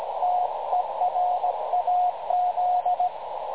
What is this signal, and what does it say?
Morse code, heard as a keyed tone, about 19 wpm, 720 Hz, with radio noise. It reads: TERAZ
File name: dlazaawansowanych / nagraniazpasma / websdr recording 3565.0kHz